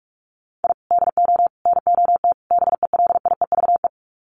Morse code, heard as a keyed tone, about 44 wpm, 700 Hz. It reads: I DO NOT BELIEVE
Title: Sets of 4 Words 45wpm